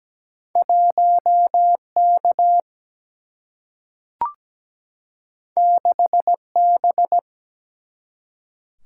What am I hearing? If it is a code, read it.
Morse code, heard as a keyed tone, about 17 wpm, 700 Hz. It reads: 1 K 6 B
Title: Single-Letter Single-Number 17wpm